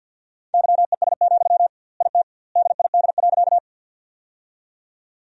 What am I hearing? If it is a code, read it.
Morse code, heard as a keyed tone, about 50 wpm, 700 Hz. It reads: YES, IT DID.
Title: 2x Word-Spacing - Sentences from Top 500 Words - 50wpm